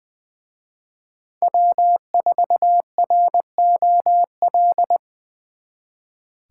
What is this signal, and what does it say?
Morse code, heard as a keyed tone, about 24 wpm, 700 Hz. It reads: W4ROL
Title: Speed Racing - USA Call Signs 20wpm